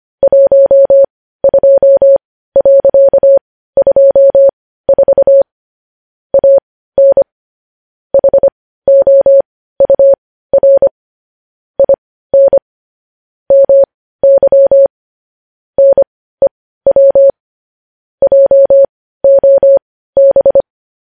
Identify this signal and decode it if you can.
Morse code, heard as a keyed tone, about 25 wpm, spaced as for 15 wpm, 550 Hz. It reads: 12.24 AN HOUR IN MY NEW JOB
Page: news-headlines-in-morse-code-at-15-wpm